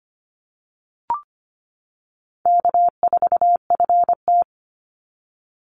Morse code, heard as a keyed tone, about 25 wpm, 700 Hz. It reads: K4FT
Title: Easy Call Signs - 25wpm